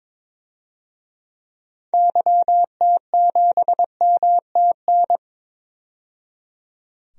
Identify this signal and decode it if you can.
Morse code, heard as a keyed tone, about 22 wpm, 700 Hz. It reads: YT7MTN
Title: DXCC-100 Call Signs - 6 Characters - 22wpm